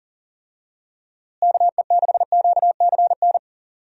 Morse code, heard as a keyed tone, about 38 wpm, 700 Hz. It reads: KE6QCN
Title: USA Call Signs - 6 Characters - Repeated 2x - 40wpm